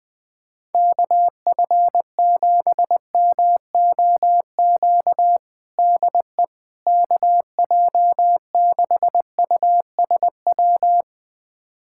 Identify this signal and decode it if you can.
Morse code, heard as a keyed tone, about 20 wpm, 700 Hz. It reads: KF7MOQ DE KJ6USW
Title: USA Callsign DE Callsign - 6 Characters - 20wpm